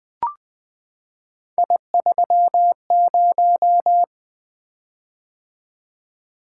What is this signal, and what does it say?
Morse code, heard as a keyed tone, about 20 wpm, 700 Hz. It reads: I30
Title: Three Letter-Number Combinations - 20wpm